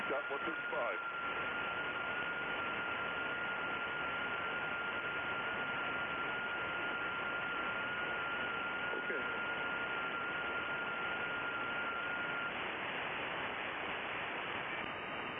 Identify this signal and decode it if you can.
FT8, with speech.